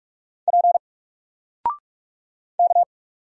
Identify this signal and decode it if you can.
Morse code, heard as a keyed tone, about 45 wpm, 700 Hz. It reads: P K
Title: Single Letter-Number Alt 45wpm